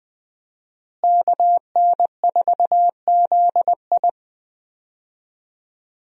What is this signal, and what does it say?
Morse code, heard as a keyed tone, about 20 wpm, 700 Hz. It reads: KN4ZI
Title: USA Call Signs - 5 Characters - Repeated 2x - 20wpm